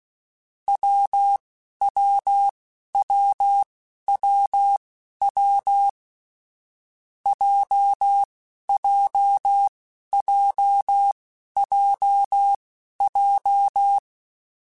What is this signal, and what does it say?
Morse code, heard as a keyed tone, about 16 wpm, spaced as for 11 wpm, 800 Hz. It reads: WWWWW JJJJJ
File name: Difficult Combos / Difficult Combos-mp3s / WPJ WPJ1 @10